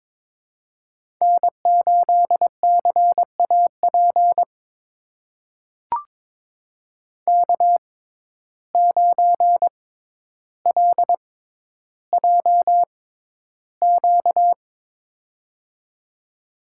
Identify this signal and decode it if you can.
Morse code, heard as a keyed tone, about 23 wpm, spaced as for 11 wpm, 700 Hz. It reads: N8CAP K9LJQ
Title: USA Call Signs - 5 Characters - 6x Character Spacing - 22wpm